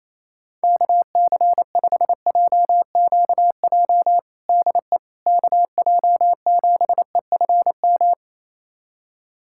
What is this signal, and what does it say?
Morse code, heard as a keyed tone, about 28 wpm, 700 Hz. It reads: KC5JQJ DE KJ7EFM
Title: USA Callsign DE Callsign - 6 Characters - 28wpm